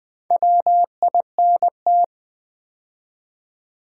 Morse code, heard as a keyed tone, about 20 wpm, 700 Hz. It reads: WINT
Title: Build A Word - 2 Letters At A Time - 6-Letter Words - 20wpm